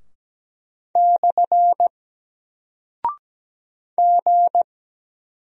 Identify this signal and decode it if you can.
Morse code, heard as a keyed tone, about 17 wpm, 700 Hz. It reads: / G
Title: Single Character (Letter:Number:Punctuation:Prosign) 17wpm